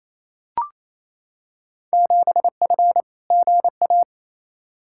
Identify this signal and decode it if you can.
Morse code, heard as a keyed tone, about 28 wpm, 700 Hz. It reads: 7F GA
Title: Field Day Exchanges Encore - 28wpm